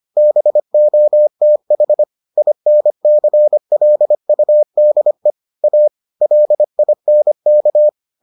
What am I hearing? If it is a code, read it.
Morse code, heard as a keyed tone, about 25 wpm, 600 Hz. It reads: BOTH INCLUDE A LINK